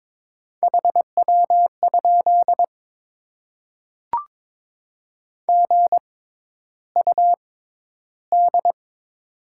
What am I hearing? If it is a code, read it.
Morse code, heard as a keyed tone, about 22 wpm, spaced as for 11 wpm, 700 Hz. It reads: HW? GUD
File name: Top 20 QSO Elements - 6x Character Spacing - 22wpm